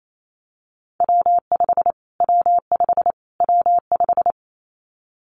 Morse code, heard as a keyed tone, about 28 wpm, 700 Hz. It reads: W5 W5 W5
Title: Callsign Prefixes for Top 25 DXCC - Repeated 3x - 28wpm